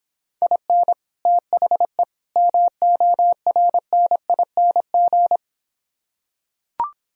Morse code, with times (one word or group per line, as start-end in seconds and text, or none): 0.4-0.9 IN
1.3-2.0 THE
2.4-5.4 MORNING